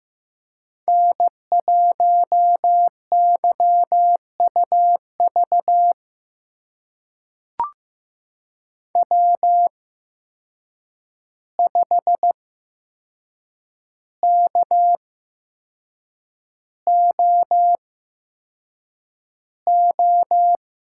Morse code, heard as a keyed tone, about 15 wpm, spaced as for 6 wpm, 700 Hz. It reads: N1YUV W5KOO
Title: USA Call Signs - 5 Characters - 8x Character Spacing - 15wpm